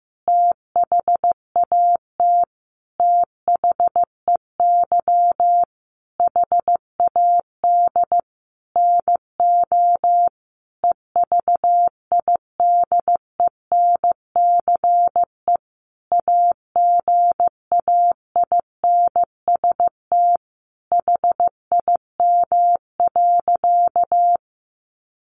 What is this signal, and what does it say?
Morse code, heard as a keyed tone, about 15 wpm, 700 Hz. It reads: THAT THEY HAD NO EVIDENCE AGAINST HIM.
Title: Sentences from Top 1300 Words 15wpm